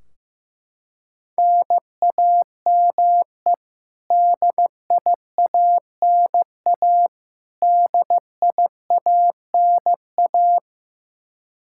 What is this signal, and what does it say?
Morse code, heard as a keyed tone, about 15 wpm, 700 Hz. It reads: NAME DIANA DIANA
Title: Practice QSO - Exchange 1 - Beginning - 15wpm